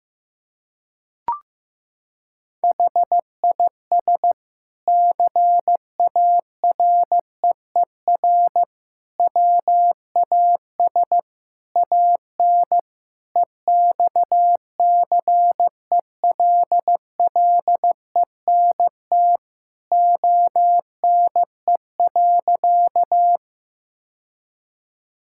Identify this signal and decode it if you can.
Morse code, heard as a keyed tone, about 15 wpm, 700 Hz. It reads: HIS CAREER WAS AN EXCELLENT ONE.
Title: Sentences from Top 1000 Words - Review Part 6 - 15wpm